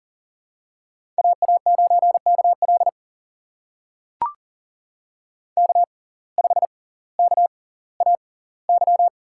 Morse code, heard as a keyed tone, about 38 wpm, spaced as for 20 wpm, 700 Hz. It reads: AA9KL K5KAY
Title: USA Call Signs - 5 Characters - 6x Character Spacing - 40wpm